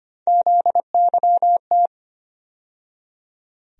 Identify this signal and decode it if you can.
Morse code, heard as a keyed tone, about 25 wpm, 700 Hz. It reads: ZYT
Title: USA Call Sign Postfixes - 3 Letters 25wpm